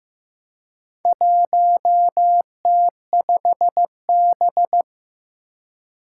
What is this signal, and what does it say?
Morse code, heard as a keyed tone, about 15 wpm, 700 Hz. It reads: 1T5B
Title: Four Challenging Letter-Number Combinations 15wpm